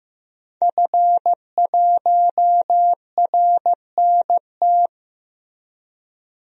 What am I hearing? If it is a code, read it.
Morse code, heard as a keyed tone, about 15 wpm, 700 Hz. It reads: F1RNT
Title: DXCC-100 Call Signs - 5 Characters - 15wpm